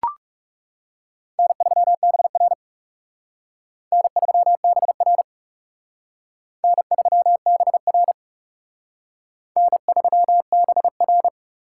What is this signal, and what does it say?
Morse code, heard as a keyed tone, about 37 wpm, 700 Hz. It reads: N3BR N3BR N3BR N3BR
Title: Speed Racing - USA 1x2 Call Signs 30wpm